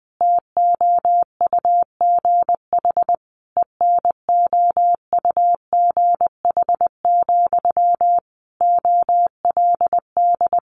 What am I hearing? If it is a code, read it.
Morse code, heard as a keyed tone, about 20 wpm, 700 Hz. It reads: TOUGH ENOUGH, OLD
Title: Sentences from Top 1200 Words 20wpm